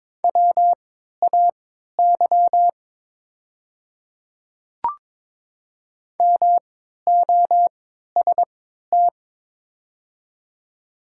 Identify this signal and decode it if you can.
Morse code, heard as a keyed tone, about 22 wpm, spaced as for 12 wpm, 700 Hz. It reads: WAY MOST
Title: 3x Character Spacing - Top 100 Words - 22wpm